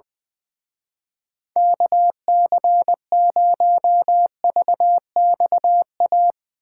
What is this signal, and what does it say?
Morse code, heard as a keyed tone, about 20 wpm, 700 Hz. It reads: KC0VXA
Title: Speed Racing - USA 2x3 Call Signs 20wpm